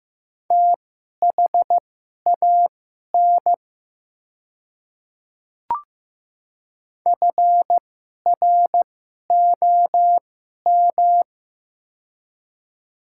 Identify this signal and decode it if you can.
Morse code, heard as a keyed tone, about 15 wpm, spaced as for 11 wpm, 700 Hz. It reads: THAN FROM
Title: 2x Character Spacing - Top 100 Words - 15wpm